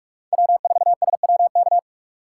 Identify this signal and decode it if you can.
Morse code, heard as a keyed tone, about 42 wpm, 700 Hz. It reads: W4SWK
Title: USA Call Signs - 3x Character Spacing - 45wpm